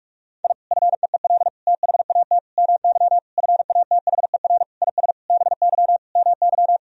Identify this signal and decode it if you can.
Morse code, heard as a keyed tone, about 41 wpm, 700 Hz. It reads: I FEEL THAT MY FATHER IS BY MY
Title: Sentences from Top 1000 Words - Review Part 1 - 45wpm